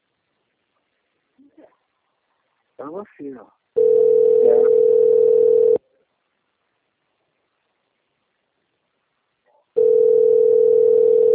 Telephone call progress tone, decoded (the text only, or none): ringback tone